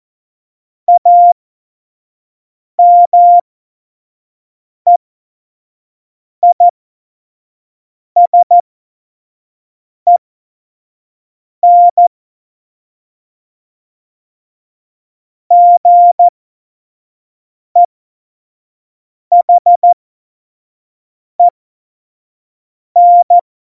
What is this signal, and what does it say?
Morse code, heard as a keyed tone, about 14 wpm, spaced as for 5 wpm, 700 Hz. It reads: AMEISEN GEHEN
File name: Morsekurs / Witze / Witz 17.Juli 2018, 25 F70